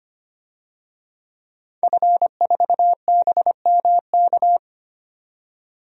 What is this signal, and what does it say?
Morse code, heard as a keyed tone, about 25 wpm, 700 Hz. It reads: F4BMK
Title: DXCC-100 Call Signs - 5 Characters - 25wpm